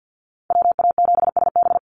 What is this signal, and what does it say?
Morse code, heard as a keyed tone, about 50 wpm, 700 Hz. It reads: WA7SB
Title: NA SOTA Chaser Callsigns 50wpm